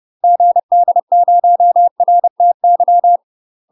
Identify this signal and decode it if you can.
Morse code, heard as a keyed tone, about 30 wpm, 700 Hz. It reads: GD0RTY
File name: UK MASTER QSOs TRAINING / 30WPM / UK MASTER QSOs 0003